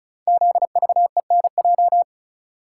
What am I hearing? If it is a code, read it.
Morse code, heard as a keyed tone, about 36 wpm, 700 Hz. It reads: ZVENJ
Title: Code Groups - 5 Letters - 35wpm